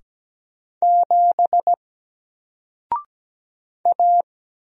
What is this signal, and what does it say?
Morse code, heard as a keyed tone, about 17 wpm, 700 Hz. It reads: 7 A
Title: Single Letter-Number Alt 17wpm